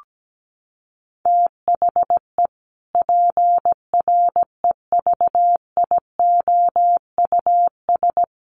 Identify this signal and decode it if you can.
Morse code, heard as a keyed tone, about 17 wpm, 700 Hz. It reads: THE PREVIOUS
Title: Sentences from Top 1500 Words 17wpm